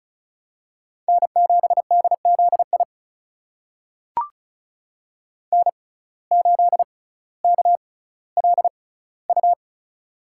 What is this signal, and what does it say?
Morse code, heard as a keyed tone, about 36 wpm, spaced as for 18 wpm, 700 Hz. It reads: N7DZI N8KLU
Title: USA Call Signs - 5 Characters - 6x Character Spacing - 35wpm